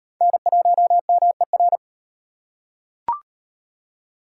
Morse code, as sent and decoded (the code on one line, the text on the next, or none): -. .---- -- . .-.
N1MER